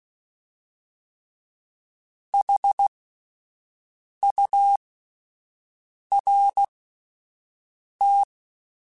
Morse code, heard as a keyed tone, about 16 wpm, spaced as for 5 wpm, 800 Hz. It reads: HURT
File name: Alphabetical / Alphabetical mp3s / Hs @5